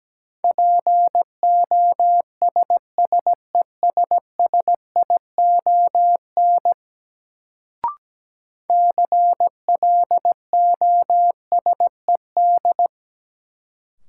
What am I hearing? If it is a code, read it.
Morse code, heard as a keyed tone, about 17 wpm, 700 Hz. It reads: POSSESSION CLOSED
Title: Sentences from Top 1400 Words - New Word Drill - 17wpm